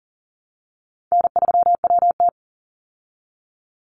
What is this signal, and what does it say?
Morse code, heard as a keyed tone, about 40 wpm, 700 Hz. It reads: N3WT
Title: Easy Call Signs Repeated 2x - 40wpm